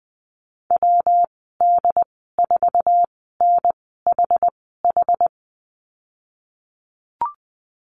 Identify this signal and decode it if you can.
Morse code, heard as a keyed tone, about 19 wpm, spaced as for 14 wpm, 700 Hz. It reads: WD4NHH
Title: 128 Lesson 21g - Number 6 - Call Signs 20wpm